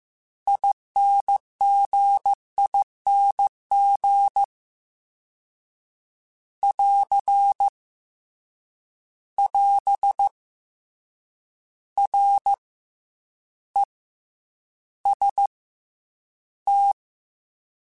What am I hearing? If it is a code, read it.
Morse code, heard as a keyed tone, about 15 wpm, spaced as for 9 wpm, 800 Hz. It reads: INGING <AR> <AS> REST
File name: Combinations / Combinations-mp3s / Combinations@13